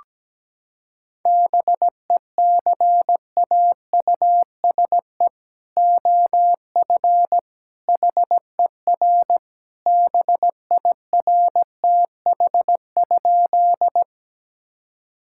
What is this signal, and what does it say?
Morse code, heard as a keyed tone, about 17 wpm, 700 Hz. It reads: BECAUSE OF HER BIRTH?
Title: Sentences from Top 1000 Words - Review Part 5 - 17wpm